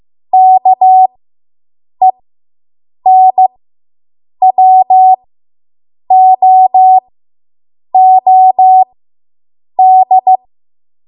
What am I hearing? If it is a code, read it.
Morse code, heard as a keyed tone, about 15 wpm, spaced as for 7 wpm, 750 Hz. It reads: KENWOOD